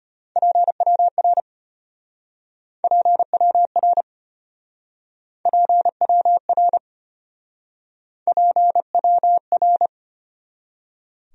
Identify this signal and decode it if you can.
Morse code, heard as a keyed tone, about 33 wpm, 700 Hz. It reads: PWR PWR PWR PWR